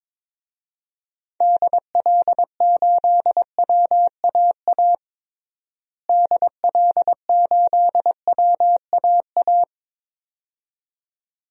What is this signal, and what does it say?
Morse code, heard as a keyed tone, about 22 wpm, 700 Hz. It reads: DL8WAA DL8WAA
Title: DXCC-100 Call Signs - 6 Characters - Repeated 2x - 22wpm